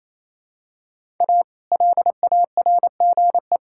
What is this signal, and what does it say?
Morse code, heard as a keyed tone, about 28 wpm, 700 Hz. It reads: A LARGE
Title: Pangram Challenge v2 - 28wpm